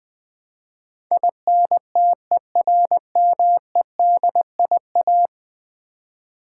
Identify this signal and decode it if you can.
Morse code, heard as a keyed tone, about 20 wpm, 700 Hz. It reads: INTERMEDIA
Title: Build A Word - 5 Letters At A Time - 12-Letter Words - 20wpm